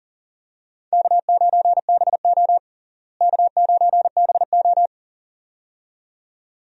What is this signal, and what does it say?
Morse code, heard as a keyed tone, about 40 wpm, 700 Hz. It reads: K9BO K9BO
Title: USA Call Signs - 4 Characters - Repeated 2x - 40wpm